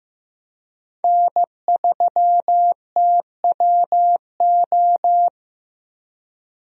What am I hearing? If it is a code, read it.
Morse code, heard as a keyed tone, about 15 wpm, 700 Hz. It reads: N3TWO
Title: USA Call Signs - 5 Characters - Repeated 2x - 15wpm